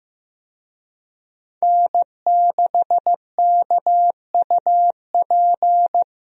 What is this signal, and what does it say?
Morse code, heard as a keyed tone, about 15 wpm, 700 Hz. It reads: N6KUP
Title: USA Call Signs - 5 Characters - Repeated 3x - 15wpm